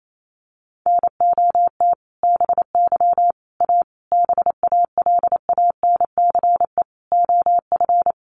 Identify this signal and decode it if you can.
Morse code, heard as a keyed tone, about 28 wpm, 700 Hz. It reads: NOT BY A BALANCE OF
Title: Sentences from Top 900 Words 28wpm